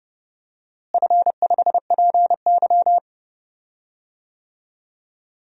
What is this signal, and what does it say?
Morse code, heard as a keyed tone, about 31 wpm, 700 Hz. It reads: F5PY
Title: DXCC-100 Call Signs - 4 Characters - Repeated 2x - 30wpm